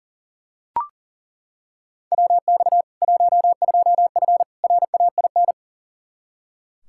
Morse code, heard as a keyed tone, about 40 wpm, 700 Hz. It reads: WX 12F RAIN
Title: K1USN SST QSO Practice 40wpm